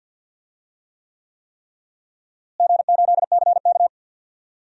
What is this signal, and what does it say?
Morse code, heard as a keyed tone, about 50 wpm, 700 Hz. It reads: G8CK
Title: DXCC-100 Call Signs - 4 Characters - Repeated 3x - 50wpm